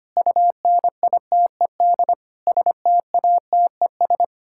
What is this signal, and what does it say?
Morse code, heard as a keyed tone, about 25 wpm, 700 Hz. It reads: UNITED STATES
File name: Sets of 9 Words - 25wpm